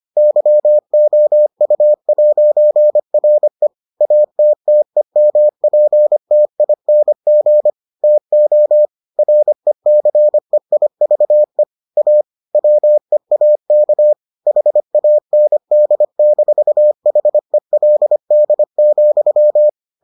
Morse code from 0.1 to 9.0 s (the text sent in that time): YOU'RE ATTEMPTING TO